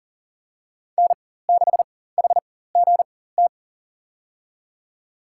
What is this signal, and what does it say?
Morse code, heard as a keyed tone, about 38 wpm, spaced as for 18 wpm, 700 Hz. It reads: N6HGT
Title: USA Call Signs - 4x Character Spacing - 40wpm